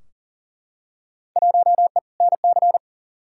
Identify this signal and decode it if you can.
Morse code, heard as a keyed tone, about 40 wpm, 700 Hz. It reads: 1E NC
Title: Field Day Exchanges - 40wpm